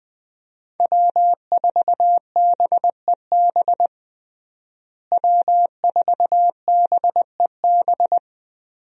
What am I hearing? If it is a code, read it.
Morse code, heard as a keyed tone, about 20 wpm, 700 Hz. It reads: W4BEB W4BEB
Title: USA Call Signs - 5 Characters - Repeated 2x - 20wpm